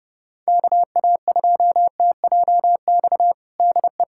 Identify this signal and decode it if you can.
Morse code, heard as a keyed tone, about 30 wpm, 700 Hz. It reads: KA2TJX DE
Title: USA Callsign DE Callsign 30wpm